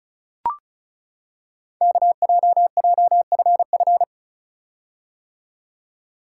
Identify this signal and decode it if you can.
Morse code, heard as a keyed tone, about 36 wpm, 700 Hz. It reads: KJJFF
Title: Code Groups - 5 Letters - 35wpm